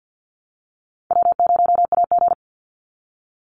Morse code, heard as a keyed tone, about 50 wpm, 700 Hz. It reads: W0AG
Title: Call Signs - 50wpm